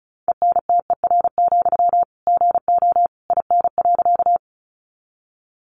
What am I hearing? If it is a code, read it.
Morse code, heard as a keyed tone, about 34 wpm, 700 Hz. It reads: ENTER, GO IN.
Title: Sentences from Top 1300 Words 35wpm